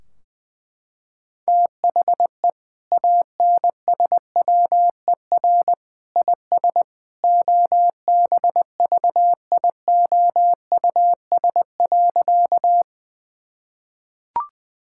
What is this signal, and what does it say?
Morse code, heard as a keyed tone, about 20 wpm, 700 Hz. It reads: THE ANSWER IS OBVIOUS.
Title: Sentences from Top 1600 Words 20wpm